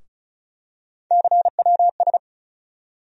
Morse code, heard as a keyed tone, about 34 wpm, 700 Hz. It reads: CWS
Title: USA Call Sign Postfixes - 3 Letters 35wpm